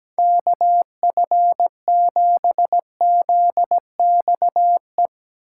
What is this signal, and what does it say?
Morse code, heard as a keyed tone, about 18 wpm, 700 Hz. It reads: KF7ZXE
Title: Speed Racing - USA 2x3 Call Signs 17wpm